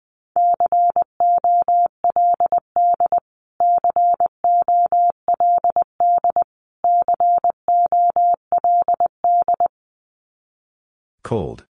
Morse code, heard as a keyed tone, about 20 wpm, 700 Hz. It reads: COLD COLD COLD
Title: Top 500 Words Repeated 20wpm